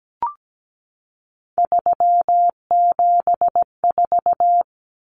Morse code, heard as a keyed tone, about 17 wpm, 700 Hz. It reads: 374
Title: Numbers - 3-Digit 17wpm